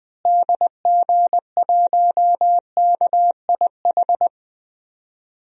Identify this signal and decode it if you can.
Morse code, heard as a keyed tone, about 20 wpm, 700 Hz. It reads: DG1KIH
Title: DXCC-100 Call Signs - 6 Characters - Repeated 3x - 20wpm